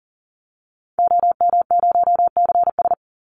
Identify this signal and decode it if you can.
Morse code, heard as a keyed tone, about 40 wpm, 700 Hz. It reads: OM0CS
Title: Call Signs Repeated 3x - 40wpm